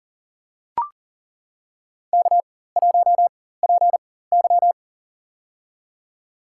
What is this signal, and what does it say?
Morse code, heard as a keyed tone, about 40 wpm, spaced as for 19 wpm, 700 Hz. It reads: K1PY
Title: USA Call Signs - 4 Characters - 4x Character Spacing - 40wpm